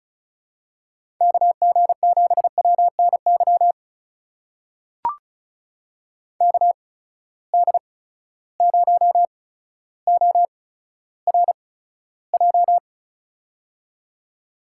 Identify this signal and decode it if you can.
Morse code, heard as a keyed tone, about 36 wpm, spaced as for 15 wpm, 700 Hz. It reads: KG7WNY KD0ORJ
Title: USA Call Signs - 6 Characters - 8x Character Spacing - 35wpm